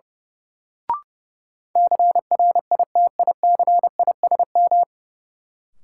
Morse code, heard as a keyed tone, about 30 wpm, 700 Hz. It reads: CRITICISM
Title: Sentences from Top 1300 Words - New Word Drill - 30wpm